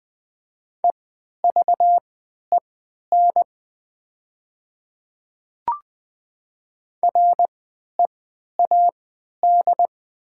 Morse code, heard as a keyed tone, about 20 wpm, spaced as for 11 wpm, 700 Hz. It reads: EVEN READ
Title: 3x Character Spacing - Top 200 Words - 20wpm